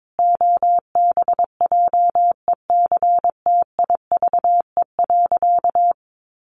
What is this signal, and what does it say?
Morse code, heard as a keyed tone, about 22 wpm, 700 Hz. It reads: OBJECTIVE.
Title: Sentences from Top 1200 Words 22wpm